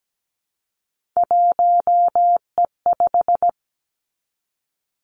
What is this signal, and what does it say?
Morse code, heard as a keyed tone, about 17 wpm, 700 Hz. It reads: 1E5
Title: Three Challenging Letter-Number Combinations 17wpm